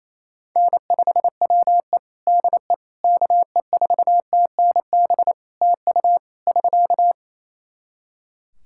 Morse code, heard as a keyed tone, about 28 wpm, 700 Hz. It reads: N5WE DE KE4TNB TU <SK>
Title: K1USN SST QSO Practice 28wpm